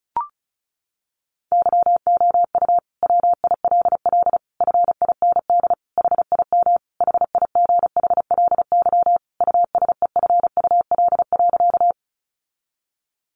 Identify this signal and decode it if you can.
Morse code, heard as a keyed tone, about 34 wpm, 700 Hz. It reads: YOU WILL FIND HIM HIGHLY USEFUL.
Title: Sentences from Top 1100 Words 35wpm